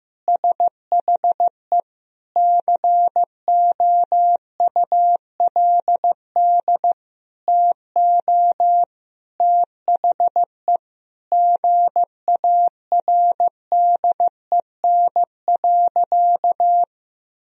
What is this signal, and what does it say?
Morse code, heard as a keyed tone, about 16 wpm, 700 Hz. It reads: SHE COULD TO THE GARDEN.
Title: Sentences from Top 1000 Words - Review Part 4 - 15wpm